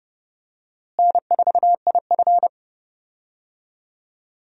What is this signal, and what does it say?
Morse code, heard as a keyed tone, about 30 wpm, 700 Hz. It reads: N4IF
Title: USA Call Signs - 4 Characters - Repeated 2x - 30wpm